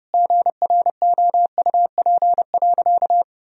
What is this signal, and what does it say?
Morse code, heard as a keyed tone, about 32 wpm, 700 Hz. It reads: GROUP.
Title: Sentences from Top 1000 Words - Review Part 1 - 30wpm